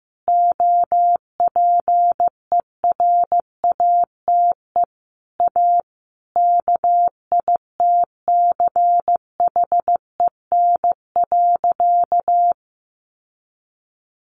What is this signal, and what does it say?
Morse code, heard as a keyed tone, about 15 wpm, 700 Hz. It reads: OPERATE A KITCHEN.
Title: Sentences from Top 1500 Words 15wpm